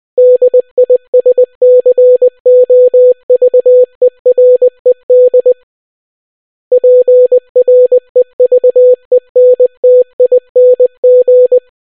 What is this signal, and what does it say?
Morse code, heard as a keyed tone, about 20 wpm, 500 Hz. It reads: DISCOVERED PREVENTING